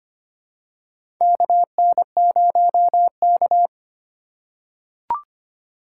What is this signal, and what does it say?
Morse code, heard as a keyed tone, about 25 wpm, 700 Hz. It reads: KN0K
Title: USA Call Signs - 4 Characters - 8x Character Spacing - 25wpm